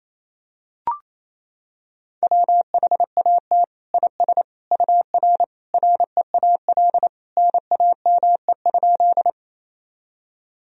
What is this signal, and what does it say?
Morse code, heard as a keyed tone, about 28 wpm, 700 Hz. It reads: WHAT IS UR REAL NAME?
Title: Sentences from Top 300 Words and Abbreviations 28wpm